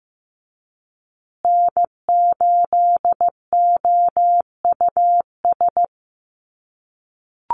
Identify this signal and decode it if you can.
Morse code, heard as a keyed tone, about 15 wpm, 700 Hz. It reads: N8OUS
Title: USA Call Signs Repeated 3x 15wpm